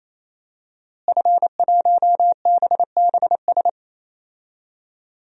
Morse code, heard as a keyed tone, about 28 wpm, 700 Hz. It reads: F1BBS
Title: DXCC-100 Call Signs - 5 Characters - Repeated 3x - 28wpm